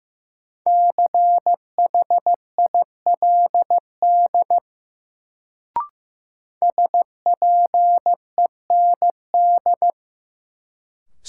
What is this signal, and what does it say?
Morse code, heard as a keyed tone, about 15 wpm, 700 Hz. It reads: CHILD SPEND
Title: Sentences from Top 800 Words - New Word Drill - 15wpm